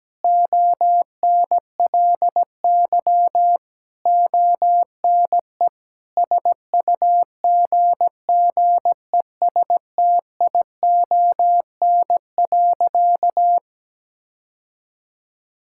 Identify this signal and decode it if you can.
Morse code, heard as a keyed tone, about 17 wpm, 700 Hz. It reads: ONLY ONE SUGGESTION.
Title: Sentences from Top 1600 Words 17wpm